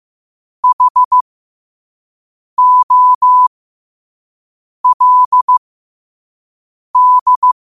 Morse code, heard as a keyed tone, about 15 wpm, spaced as for 5 wpm, 1000 Hz. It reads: HOLD
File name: qotdfiles / 5wpm